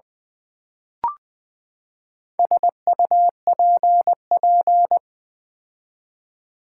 Morse code, heard as a keyed tone, about 20 wpm, 700 Hz. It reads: SUPP
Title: Build A Word - 4 Letters At A Time - 10-Letter Words - 20wpm